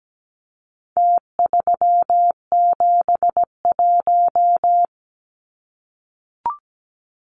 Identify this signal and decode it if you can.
Morse code, heard as a keyed tone, about 17 wpm, 700 Hz. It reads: T371
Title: Numbers - Contest Style - 4-Digit 17wpm